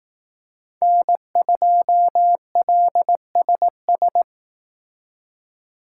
Morse code, heard as a keyed tone, about 18 wpm, 700 Hz. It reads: N2LSS
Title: Speed Racing - USA 1x3 Call Signs 15wpm